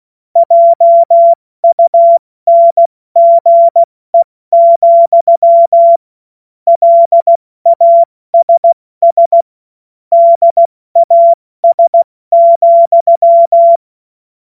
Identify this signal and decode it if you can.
Morse code, heard as a keyed tone, about 16 wpm, 680 Hz. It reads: JUNGE, LASS DAS,
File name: Morsekurs / Witze / Witz 17. August 2021, 70 F80